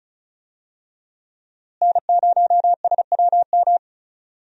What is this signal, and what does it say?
Morse code, heard as a keyed tone, about 36 wpm, 700 Hz. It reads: N0SWM
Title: USA Call Signs - 5 Characters - Repeated 2x - 35wpm